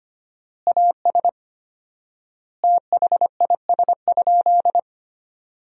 Morse code, heard as a keyed tone, about 25 wpm, 700 Hz. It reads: AS THIS?
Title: 4x Word-Spacing - Sentences from Top 400 Words - 25wpm